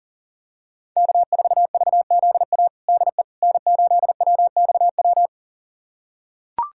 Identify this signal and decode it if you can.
Morse code, heard as a keyed tone, about 40 wpm, 700 Hz. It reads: K4VZA DE N8WXW
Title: USA Callsign DE Callsign - 5 Characters - 40wpm